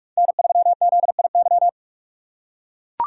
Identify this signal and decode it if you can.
Morse code, heard as a keyed tone, about 42 wpm, 700 Hz. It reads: N3ZIY